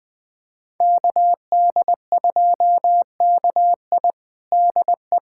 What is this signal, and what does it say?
Morse code, heard as a keyed tone, about 20 wpm, 700 Hz. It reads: KD2KI DE
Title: USA Callsign DE Callsign 20wpm